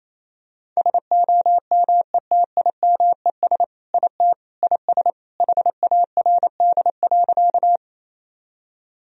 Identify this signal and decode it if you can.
Morse code, heard as a keyed tone, about 28 wpm, 700 Hz. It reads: SOMETIMES IT IS HARD.
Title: Sentences from Top 1000 Words - Review Part 3 - 28wpm